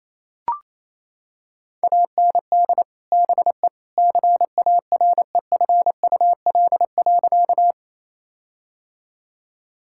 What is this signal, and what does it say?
Morse code, heard as a keyed tone, about 28 wpm, 700 Hz. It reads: AND BE CAREFUL.
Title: Sentences from Top 1600 Words 28wpm